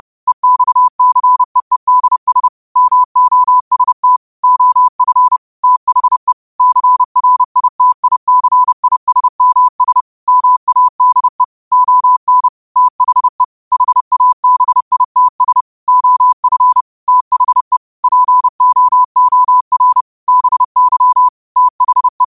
Morse code, heard as a keyed tone, about 30 wpm, 1000 Hz. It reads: EXCEEDS MOST OF THE CRITICISMS MADE ON THE HABITS OF THE POOR BY THE